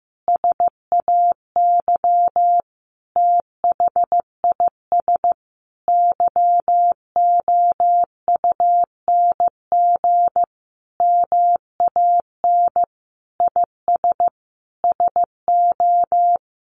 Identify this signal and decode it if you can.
Morse code, heard as a keyed tone, about 15 wpm, 700 Hz. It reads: SAY THIS YOUNG MAN IS SO
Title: Sentences from Top 1100 Words 15wpm